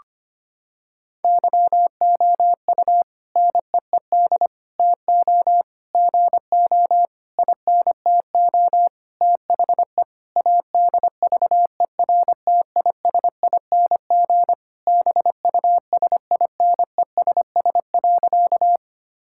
Morse code, heard as a keyed tone, about 25 wpm, 700 Hz. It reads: YOU NEED TO GO INTO THE ADVERTISING BUSINESS.
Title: Sentences from Top 1000 Words - Review Part 6 - 25wpm